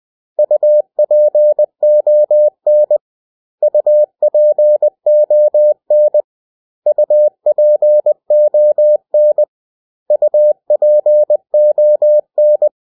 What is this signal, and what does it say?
Morse code, heard as a keyed tone, about 20 wpm, 600 Hz. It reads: UPON UPON UPON UPON
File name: cw / 100 Common English Words